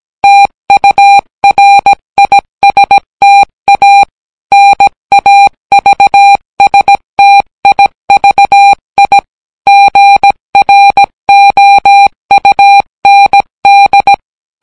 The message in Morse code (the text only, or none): TURISTA NAVSTIVI GROUND